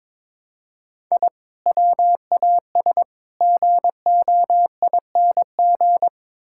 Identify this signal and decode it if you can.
Morse code, heard as a keyed tone, about 22 wpm, 700 Hz. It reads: I WAS GOING
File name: practice / 22wpm / Build A Phrase - 5-Word Phrases - 22wpm